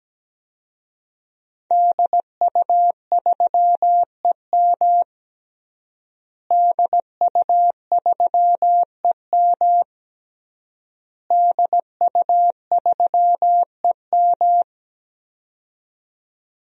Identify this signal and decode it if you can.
Morse code, heard as a keyed tone, about 17 wpm, 700 Hz. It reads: DU3EM DU3EM DU3EM
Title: DXCC-100 Call Signs - 5 Characters - Repeated 3x - 17wpm